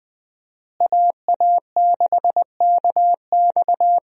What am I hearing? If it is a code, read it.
Morse code, heard as a keyed tone, about 20 wpm, 700 Hz. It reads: AA6KX
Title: USA Call Signs - 5 Characters - Repeated 2x - 20wpm